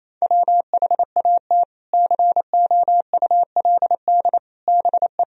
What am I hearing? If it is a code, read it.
Morse code, heard as a keyed tone, about 28 wpm, 700 Hz. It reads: WHAT COULD BE
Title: Sentences from Top 400 Words and Abbreviations 28wpm